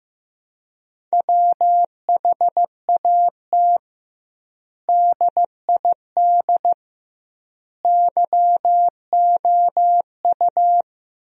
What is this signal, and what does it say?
Morse code, heard as a keyed tone, about 15 wpm, 700 Hz. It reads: WHAT DID YOU
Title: 2x Word-Spacing - Sentences from Top 100 Words - 15wpm